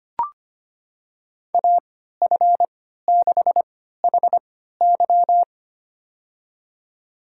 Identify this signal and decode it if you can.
Morse code, heard as a keyed tone, about 25 wpm, spaced as for 14 wpm, 700 Hz. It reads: AF6HY
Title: USA Call Signs - 5 Characters - 3x Character Spacing - 25wpm